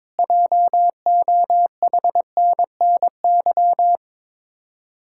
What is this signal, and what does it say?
Morse code, heard as a keyed tone, about 22 wpm, 700 Hz. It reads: JOHNNY
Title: Practice QSO - Exchange 1 - Beginning - 22wpm